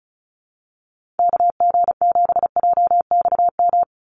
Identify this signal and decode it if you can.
Morse code, heard as a keyed tone, about 34 wpm, 700 Hz. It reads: KG7JXM
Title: USA Call Signs 35wpm